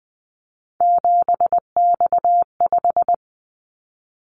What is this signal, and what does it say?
Morse code, heard as a keyed tone, about 20 wpm, 700 Hz. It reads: 7X5
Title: Callsign Prefixes for Top 100 DXCC - 20wpm